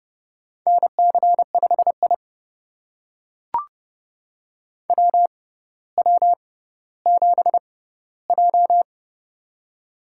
Morse code, heard as a keyed tone, about 30 wpm, spaced as for 15 wpm, 690 Hz. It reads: NC5I WW7J
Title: USA Call Signs - 4 Characters - 6x Character Spacing - 30wpm